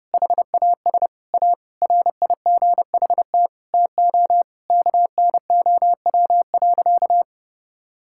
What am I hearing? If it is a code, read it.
Morse code, heard as a keyed tone, about 31 wpm, 700 Hz. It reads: HAS A RIGHT TO KNOW.
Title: Sentences from Top 1000 Words - Review Part 6 - 30wpm